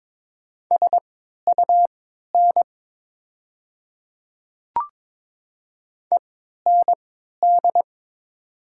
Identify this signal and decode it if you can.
Morse code, heard as a keyed tone, about 21 wpm, spaced as for 12 wpm, 700 Hz. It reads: SUN END
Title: 3x Character Spacing - Top 300 Words - 22wpm